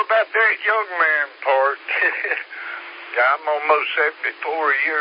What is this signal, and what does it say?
Emergency Alert System (SAME) tones, heard as harsh buzzing bursts, with speech. It is heard from 0.9 to 4.4 s.